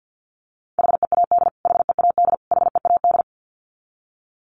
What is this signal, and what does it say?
Morse code, heard as a keyed tone, about 50 wpm, 700 Hz. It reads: HEAD HEAD HEAD
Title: Top 300 Words Repeated 50wpm